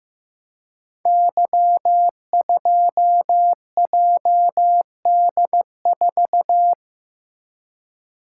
Morse code, heard as a keyed tone, about 15 wpm, 700 Hz. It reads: Y2JD4
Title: Code Groups - 5 Letter:Numbers - 15wpm